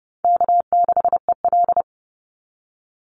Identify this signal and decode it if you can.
Morse code, heard as a keyed tone, about 30 wpm, 700 Hz. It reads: K6EL
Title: NA SOTA Chaser Callsigns 30wpm